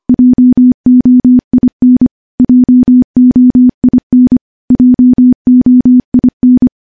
Morse code, heard as a keyed tone, about 25 wpm, 250 Hz. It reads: JOIN JOIN JOIN
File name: morse / morse-training-25WPM